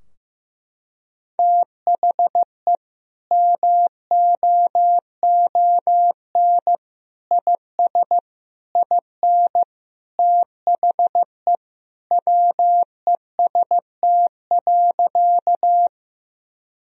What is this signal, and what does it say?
Morse code, heard as a keyed tone, about 15 wpm, 700 Hz. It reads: THE MOON IS IN THE WEST.
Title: Sentences from Top 1000 Words - Review Part 3 - 15wpm